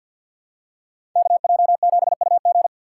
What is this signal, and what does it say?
Morse code, heard as a keyed tone, about 53 wpm, 700 Hz. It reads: KJ7UG
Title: USA Call Signs - 5 Characters - 6x Character Spacing - 50wpm